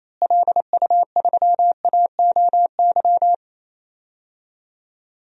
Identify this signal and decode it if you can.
Morse code, heard as a keyed tone, about 28 wpm, 700 Hz. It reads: LU3AOY
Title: DXCC-100 Call Signs - 6 Characters - Repeated 3x - 28wpm